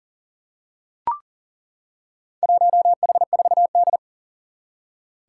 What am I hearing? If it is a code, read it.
Morse code, heard as a keyed tone, about 40 wpm, 700 Hz. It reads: 1H4D
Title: Four Challenging Letter-Number Combinations 40wpm